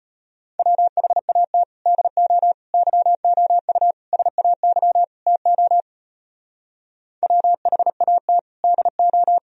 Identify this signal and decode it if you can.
Morse code, heard as a keyed tone, about 32 wpm, 700 Hz. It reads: WHAT DO YOU SAY TO WHAT DO